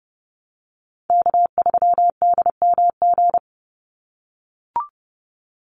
Morse code, heard as a keyed tone, about 30 wpm, 700 Hz. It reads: K3DMG
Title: Call Signs - 30wpm